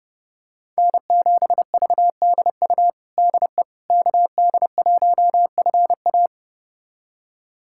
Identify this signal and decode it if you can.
Morse code, heard as a keyed tone, about 30 wpm, 700 Hz. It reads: N7VDU DE KD1FA
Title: USA Callsign DE Callsign - 5 Characters - 30wpm